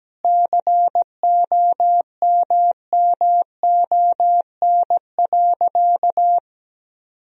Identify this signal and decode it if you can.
Morse code, heard as a keyed tone, about 17 wpm, 700 Hz. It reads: COMMON.
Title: Sentences from Top 1000 Words - Review Part 3 - 17wpm